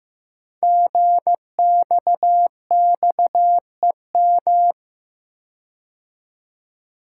Morse code, heard as a keyed tone, about 15 wpm, 700 Hz. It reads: GXXEM
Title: Code Groups - 5 Letters - 15wpm